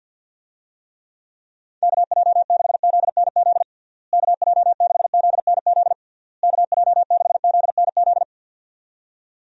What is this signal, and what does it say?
Morse code, heard as a keyed tone, about 48 wpm, 700 Hz. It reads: KJ6ZNZ KJ6ZNZ KJ6ZNZ
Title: USA Call Signs - 6 Characters - Repeated 3x - 50wpm